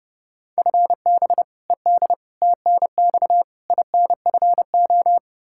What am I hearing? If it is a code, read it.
Morse code, heard as a keyed tone, about 30 wpm, 700 Hz. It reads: FB ED TNX INFO
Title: Practice QSO - Exchange 3 - More Info - 30wpm